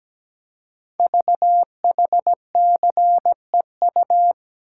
Morse code, heard as a keyed tone, about 17 wpm, 700 Hz. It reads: VHCEU
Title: Code Groups - 5 Letters - 17wpm